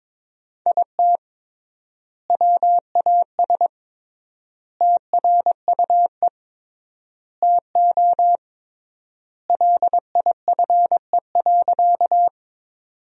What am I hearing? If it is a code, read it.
Morse code, heard as a keyed tone, about 22 wpm, 700 Hz. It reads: IT WAS TRUE TO LIFE.
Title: 3x Word-Spacing - Sentences from Top 400 Words - 22wpm